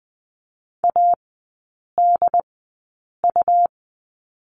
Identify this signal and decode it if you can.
Morse code, heard as a keyed tone, about 20 wpm, spaced as for 8 wpm, 700 Hz. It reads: ADU
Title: 107 Lesson 18e - Letter m - Three Characters 20wpm